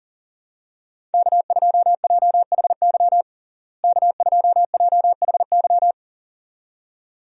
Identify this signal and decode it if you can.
Morse code, heard as a keyed tone, about 40 wpm, 700 Hz. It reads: K2JHY K2JHY
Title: USA Call Signs - 5 Characters - Repeated 2x - 40wpm